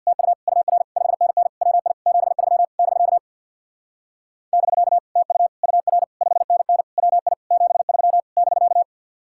Morse code, heard as a keyed tone, about 64 wpm, 700 Hz. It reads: TU UR 5NN WI 73 <BK> <BK> TU UR 5NN WI 73 <BK>